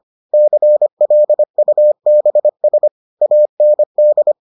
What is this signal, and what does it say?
Morse code, heard as a keyed tone, about 25 wpm, 600 Hz. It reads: CLUBS AND